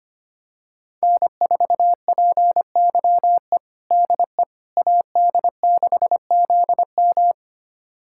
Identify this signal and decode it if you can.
Morse code, heard as a keyed tone, about 25 wpm, 700 Hz. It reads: N4PYE DE AD6ZM
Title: USA Callsign DE Callsign - 5 Characters - 25wpm